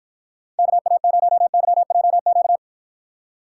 Morse code, heard as a keyed tone, about 53 wpm, 700 Hz. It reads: KA0YJQ